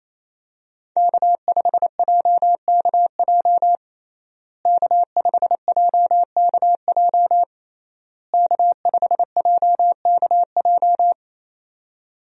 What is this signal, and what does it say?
Morse code, heard as a keyed tone, about 28 wpm, 700 Hz. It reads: K5JKJ K5JKJ K5JKJ
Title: USA Call Signs - 5 Characters - Repeated 3x - 28wpm